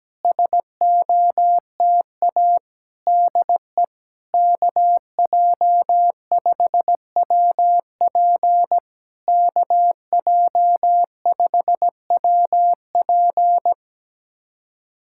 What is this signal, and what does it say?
Morse code, heard as a keyed tone, about 17 wpm, 700 Hz. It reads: SOTA DE KJ5WP KJ5WP
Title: SOTA QSO Practice 17wpm